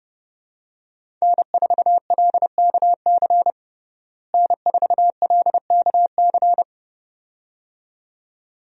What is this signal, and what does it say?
Morse code, heard as a keyed tone, about 31 wpm, 700 Hz. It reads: N4LKC N4LKC
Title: USA Call Signs - 5 Characters - Repeated 2x - 30wpm